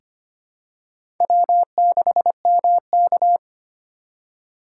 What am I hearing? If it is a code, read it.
Morse code, heard as a keyed tone, about 25 wpm, 700 Hz. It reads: W6MK
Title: USA Call Signs - 2x Character Spacing - 25wpm